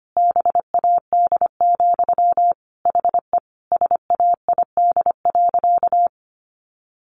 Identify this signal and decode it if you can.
Morse code, heard as a keyed tone, about 25 wpm, 700 Hz. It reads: BAD, HE SAID.
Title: Sentences from Top 700 Words 25wpm